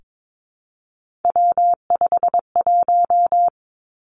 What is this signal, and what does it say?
Morse code, heard as a keyed tone, about 22 wpm, 700 Hz. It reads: W51
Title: Three Challenging Letter-Number Combinations 22wpm